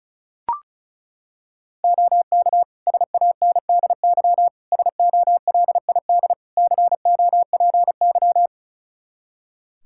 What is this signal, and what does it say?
Morse code, heard as a keyed tone, about 36 wpm, 700 Hz. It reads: OK SANDY SOLID COPY
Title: Practice QSO - Exchange 3 - More Info - 35wpm